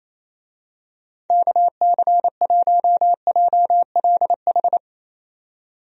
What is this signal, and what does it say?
Morse code, heard as a keyed tone, about 27 wpm, 700 Hz. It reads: KC1JLH
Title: USA Call Signs - 6 Characters - 28wpm